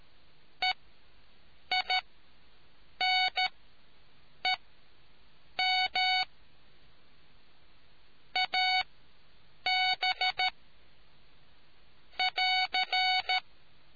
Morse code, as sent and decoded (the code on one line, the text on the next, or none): . .. -. -- / .- -... / .-.-.
EINEM AB <AR>